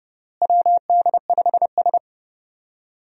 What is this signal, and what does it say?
Morse code, heard as a keyed tone, about 29 wpm, 700 Hz. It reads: WD5S
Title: USA Call Signs - 4 Characters - Repeated 2x - 30wpm